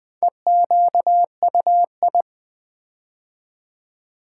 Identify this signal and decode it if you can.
Morse code, heard as a keyed tone, about 21 wpm, 700 Hz. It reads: EQUI
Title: Build A Word - 4 Letters At A Time - 10-Letter Words - 20wpm